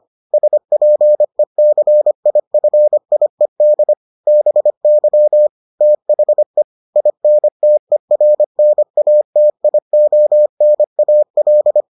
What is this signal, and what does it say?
Morse code, heard as a keyed tone, about 25 wpm, 600 Hz. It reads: SPECIFIED BY THE INTERNATIONAL